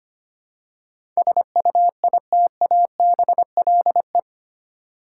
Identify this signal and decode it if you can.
Morse code, heard as a keyed tone, about 25 wpm, 700 Hz. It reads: SUITABLE
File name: Eight Letter Words - 25wpm